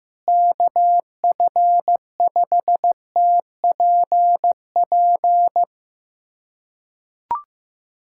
Morse code, heard as a keyed tone, about 15 wpm, 700 Hz. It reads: KF5TPP
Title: USA Callsign DE Callsign - 6 Characters - 15wpm